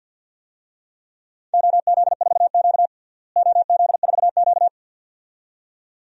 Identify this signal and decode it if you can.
Morse code, heard as a keyed tone, about 51 wpm, 700 Hz. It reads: OZ4Q OZ4Q
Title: DXCC-100 Call Signs - 4 Characters - Repeated 2x - 50wpm